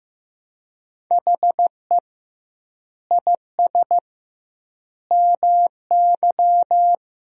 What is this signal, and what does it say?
Morse code, heard as a keyed tone, about 15 wpm, 700 Hz. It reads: HE IS MY
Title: 2x Word-Spacing - Sentences from Top 300 Words - 15wpm